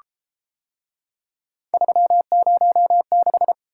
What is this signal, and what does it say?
Morse code, heard as a keyed tone, about 29 wpm, 700 Hz. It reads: 306